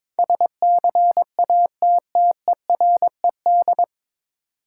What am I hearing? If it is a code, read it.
Morse code, heard as a keyed tone, about 22 wpm, 700 Hz. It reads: SCATTERED
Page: Morse Code Ninja practice 22wpm